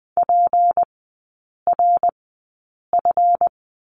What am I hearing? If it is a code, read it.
Morse code, heard as a keyed tone, about 20 wpm, spaced as for 8 wpm, 700 Hz. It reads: PRF
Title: 152 Lesson 25e - Letter p - Three Characters 20wpm